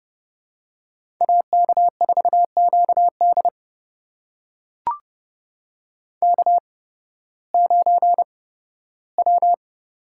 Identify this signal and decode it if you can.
Morse code, heard as a keyed tone, about 30 wpm, spaced as for 16 wpm, 700 Hz. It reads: AK4QD K9W
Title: USA Call Signs - 5 Characters - 8x Character Spacing - 30wpm